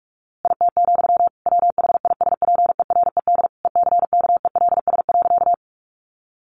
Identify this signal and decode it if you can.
Morse code, heard as a keyed tone, about 44 wpm, 700 Hz. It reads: IT, WHISPERED ECKELS.
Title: A Sound Of Thunder 45wpm